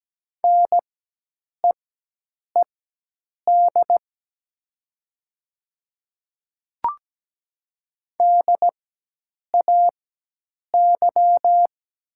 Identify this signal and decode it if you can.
Morse code, heard as a keyed tone, about 17 wpm, spaced as for 8 wpm, 700 Hz. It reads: NEED DAY
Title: 4x Character Spacing - Top 200 Words - 17wpm